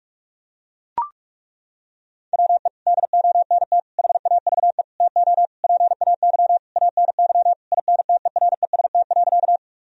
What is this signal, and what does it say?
Morse code, heard as a keyed tone, about 41 wpm, 700 Hz. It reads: WE DONT HAVE TO PAY ANY INTEREST.
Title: Sentences from Top 1000 Words - Review Part 2 - 45wpm